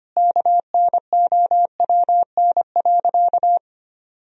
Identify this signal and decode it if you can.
Morse code, heard as a keyed tone, about 25 wpm, 700 Hz. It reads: KNOWN.
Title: Sentences from Top 1000 Words - Review Part 3 - 25wpm